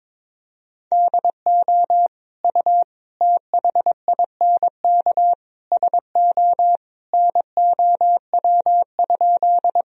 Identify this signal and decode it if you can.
Morse code, heard as a keyed tone, about 22 wpm, 700 Hz. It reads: DO U THINK SO NOW?
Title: Sentences from Top 200 Words and Abbreviations 22wpm